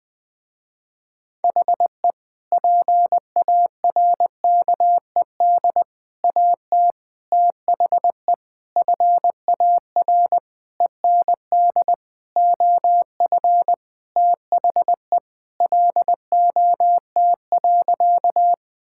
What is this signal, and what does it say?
Morse code, heard as a keyed tone, about 20 wpm, 700 Hz. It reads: HE PARKED AT THE FAR END OF THE LOT.
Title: Sentences with 5-Word Phrases - Spoken Beginning - 20wpm